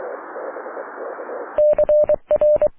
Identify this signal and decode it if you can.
Morse code, heard as a keyed tone, about 23 wpm, 600 Hz. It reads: CR
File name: wp / la8aja yu2mma aurora 05 aug 2011